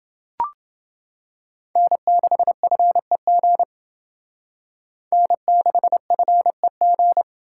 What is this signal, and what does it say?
Morse code, heard as a keyed tone, about 28 wpm, 700 Hz. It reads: N6FEG N6FEG